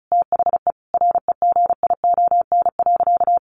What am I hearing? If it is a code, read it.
Morse code, heard as a keyed tone, about 34 wpm, 700 Hz. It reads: THE REGION.
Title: Sentences from Top 1500 Words 35wpm